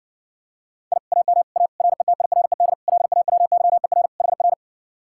Morse code, heard as a keyed tone, about 70 wpm, 700 Hz. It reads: I AM A RETIRED LAWYER HR